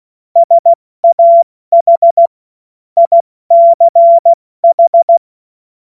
Morse code, heard as a keyed tone, about 16 wpm, 680 Hz. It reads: SAH ICH
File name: Morsekurs / Witze / Witz 31. Januar 2023 70 F80